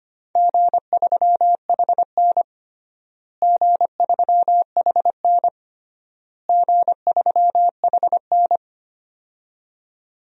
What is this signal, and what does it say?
Morse code, heard as a keyed tone, about 25 wpm, 700 Hz. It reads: G3HN G3HN G3HN
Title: DXCC-100 Call Signs - 4 Characters - Repeated 3x - 25wpm